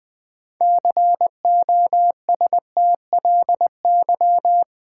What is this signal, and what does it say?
Morse code, heard as a keyed tone, about 20 wpm, 690 Hz. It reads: COSTLY
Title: Build A Word - 2 Letters At A Time - 6-Letter Words - 20wpm